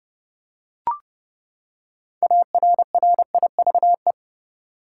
Morse code, heard as a keyed tone, about 32 wpm, 700 Hz. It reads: ARRIVE
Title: Six Letter Words - 30wpm